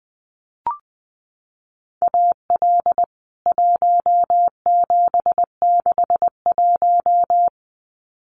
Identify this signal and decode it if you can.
Morse code, heard as a keyed tone, about 20 wpm, 700 Hz. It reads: AL 1761
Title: CWT Exchange 20wpm